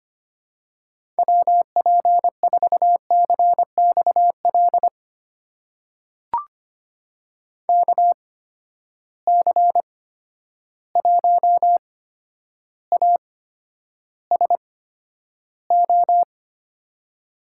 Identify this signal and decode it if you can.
Morse code, heard as a keyed tone, about 25 wpm, spaced as for 11 wpm, 700 Hz. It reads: WP4CXL KC1ASO
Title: USA Call Signs - 6 Characters - 8x Character Spacing - 25wpm